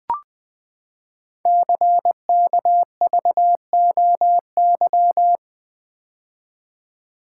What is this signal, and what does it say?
Morse code, heard as a keyed tone, about 20 wpm, 700 Hz. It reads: CKVOY
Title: Code Groups - 5 Letters - 20wpm